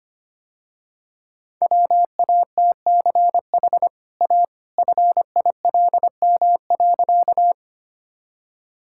Morse code, heard as a keyed tone, about 25 wpm, 700 Hz. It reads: WATCH A FILM.
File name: Easy Sentences with Hints - 25wpm